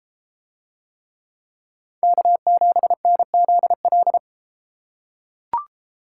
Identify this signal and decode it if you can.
Morse code, heard as a keyed tone, about 32 wpm, 700 Hz. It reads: K7NZL